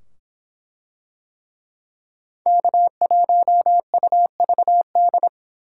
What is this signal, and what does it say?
Morse code, heard as a keyed tone, about 23 wpm, 700 Hz. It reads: K1UVD